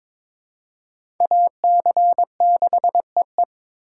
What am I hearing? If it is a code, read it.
Morse code, heard as a keyed tone, about 22 wpm, 700 Hz. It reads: AC6EE
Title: USA Call Signs - 5 Characters - 4x Character Spacing - 22wpm